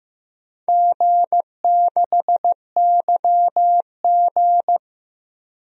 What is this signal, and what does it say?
Morse code, heard as a keyed tone, about 15 wpm, 700 Hz. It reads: G6YG
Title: DXCC-100 Call Signs - 4 Characters - Repeated 2x - 15wpm